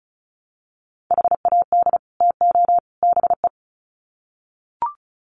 Morse code, heard as a keyed tone, about 34 wpm, 700 Hz. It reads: HAD TO BE
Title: Sets of 3 Words 35wpm